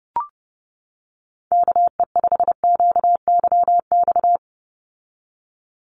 Morse code, heard as a keyed tone, about 30 wpm, 700 Hz. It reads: KE5QYX
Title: USA Call Signs 30wpm